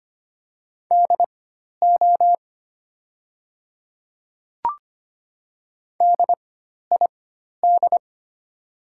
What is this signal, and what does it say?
Morse code, heard as a keyed tone, about 25 wpm, spaced as for 12 wpm, 700 Hz. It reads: DO DID
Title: 4x Character Spacing - Top 100 Words - 25wpm